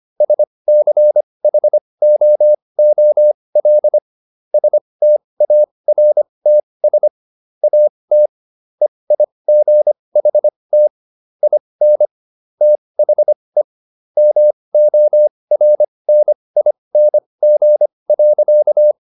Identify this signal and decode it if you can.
Morse code, heard as a keyed tone, about 25 wpm, spaced as for 20 wpm, 600 Hz. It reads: SCHOOL STARTS AT EIGHT IN THE MORNING.